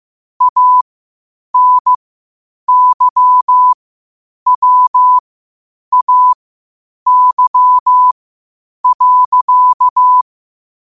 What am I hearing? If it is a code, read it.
Morse code, heard as a keyed tone, about 15 wpm, spaced as for 8 wpm, 1000 Hz. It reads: ANYWAY.
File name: qotdfiles / 8wpm